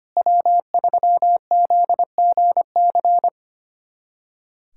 Morse code, heard as a keyed tone, about 25 wpm, 700 Hz. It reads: W3ZGC
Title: USA Callsign DE Callsign 25wpm